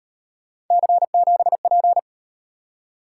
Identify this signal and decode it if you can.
Morse code, heard as a keyed tone, about 38 wpm, 700 Hz. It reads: C7P